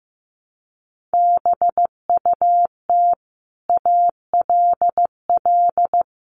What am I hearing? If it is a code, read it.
Morse code, heard as a keyed tone, about 15 wpm, 700 Hz. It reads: BUT ALL
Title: Sentences from Top 1400 Words 15wpm